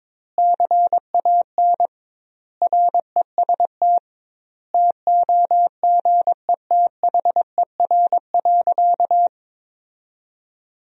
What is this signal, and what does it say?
Morse code, heard as a keyed tone, about 22 wpm, 700 Hz. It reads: CAN REST TOGETHER.
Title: 2x Word-Spacing - Sentences from Top 500 Words - 22wpm